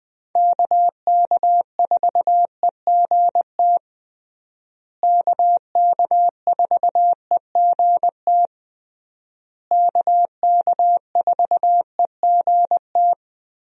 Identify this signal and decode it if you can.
Morse code, heard as a keyed tone, about 20 wpm, 700 Hz. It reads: KK4EGT KK4EGT KK4EGT
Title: USA Call Signs - 6 Characters - Repeated 3x - 20wpm